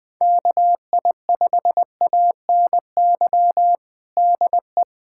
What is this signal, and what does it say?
Morse code, heard as a keyed tone, about 20 wpm, 700 Hz. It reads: KI5ANY DE
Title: USA Callsign DE Callsign 20wpm